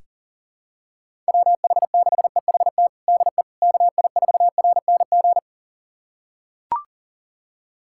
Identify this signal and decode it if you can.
Morse code, heard as a keyed tone, about 40 wpm, 700 Hz. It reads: WH6EHT DE KI4RNG
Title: USA Callsign DE Callsign 40wpm